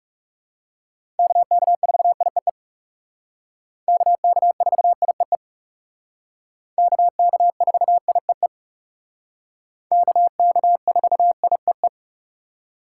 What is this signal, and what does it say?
Morse code, heard as a keyed tone, about 38 wpm, 700 Hz. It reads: KK4IEE KK4IEE KK4IEE KK4IEE